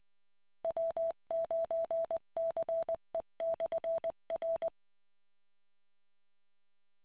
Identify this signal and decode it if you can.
Morse code, heard as a keyed tone, about 20 wpm, 660 Hz. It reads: W9CE/R